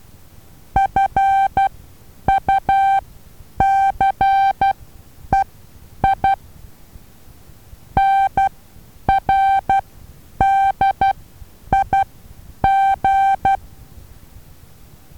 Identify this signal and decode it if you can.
Morse code, heard as a keyed tone, about 12 wpm, spaced as for 9 wpm, 780 Hz. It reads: FUCEI NRDIG
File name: Taviro HA6VA B